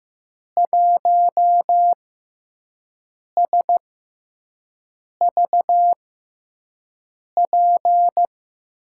Morse code, heard as a keyed tone, about 15 wpm, spaced as for 5 wpm, 700 Hz. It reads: 1SVP
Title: USA Call Signs - 6 Characters - 6x Character Spacing - 15wpm